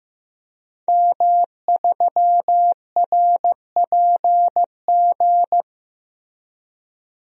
Morse code, heard as a keyed tone, about 15 wpm, 700 Hz. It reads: M3RPG
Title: DXCC-100 Call Signs - 5 Characters - Repeated 2x - 15wpm